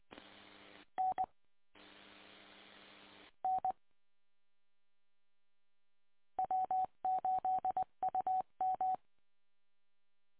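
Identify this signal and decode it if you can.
Morse code, heard as a keyed tone, about 20 wpm, 750 Hz. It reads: N N W8UM
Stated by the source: Recording from W8UM archive, 145.230 MHz.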